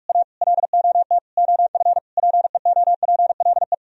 Morse code, heard as a keyed tone, about 47 wpm, 700 Hz. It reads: A LOT OF PEOPLE